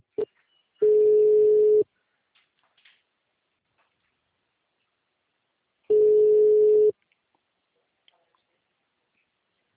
Ringback tone (Europe, 425 Hz).